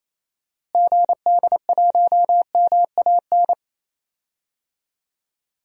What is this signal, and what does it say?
Morse code, heard as a keyed tone, about 28 wpm, 700 Hz. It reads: GD1MAN
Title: DXCC-100 Call Signs - 6 Characters - Repeated 3x - 28wpm